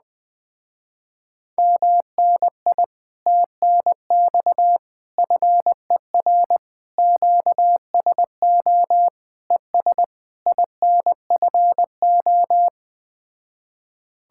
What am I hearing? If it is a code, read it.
Morse code, heard as a keyed tone, about 20 wpm, 700 Hz. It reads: MNI TNX FER QSO ES INFO